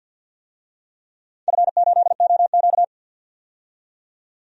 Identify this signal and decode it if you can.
Morse code, heard as a keyed tone, about 50 wpm, 700 Hz. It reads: F8OQ
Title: DXCC-100 Call Signs - 4 Characters - 50wpm